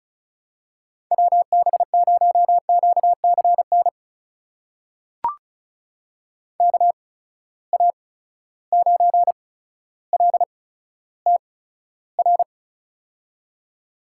Morse code, heard as a keyed tone, about 36 wpm, spaced as for 15 wpm, 700 Hz. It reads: WB0QCN KA9LTR
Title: USA Call Signs - 6 Characters - 8x Character Spacing - 35wpm